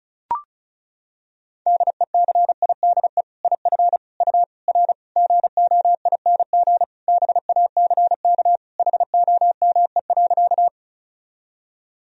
Morse code, heard as a keyed tone, about 34 wpm, 700 Hz. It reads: DECIDE IF U R GOING BACK HOME.
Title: Sentences from Top 500 Words and Abbreviations 35wpm